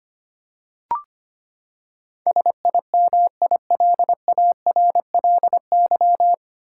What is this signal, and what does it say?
Morse code, heard as a keyed tone, about 25 wpm, 700 Hz. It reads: SIMILARLY
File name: Nine Letter Words - 25wpm